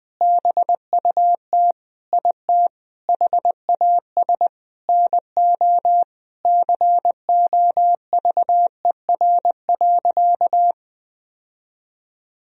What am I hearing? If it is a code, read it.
Morse code, heard as a keyed tone, about 20 wpm, 700 Hz. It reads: BUT IT HAS NO COVER.
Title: Sentences from Top 1000 Words - Review Part 1 - 20wpm